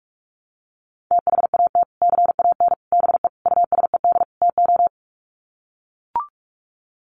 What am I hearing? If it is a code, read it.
Morse code, heard as a keyed tone, about 44 wpm, 700 Hz. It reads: THAT CAN BE USED TO
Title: Sets of 5 Words Encore 45wpm